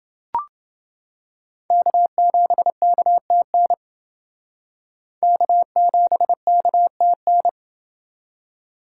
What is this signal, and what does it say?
Morse code, heard as a keyed tone, about 29 wpm, 700 Hz. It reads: K7KTN K7KTN